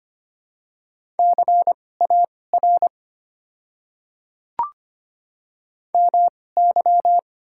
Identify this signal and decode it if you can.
Morse code, heard as a keyed tone, about 25 wpm, spaced as for 18 wpm, 700 Hz. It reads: CAR MY
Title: 2x Character Spacing - Top 300 Words - 25wpm